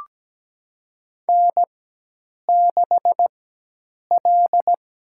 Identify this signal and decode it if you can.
Morse code, heard as a keyed tone, about 17 wpm, 700 Hz. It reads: N 6 L
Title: USA Call Signs - 4 Characters - 4x Character Spacing - 17wpm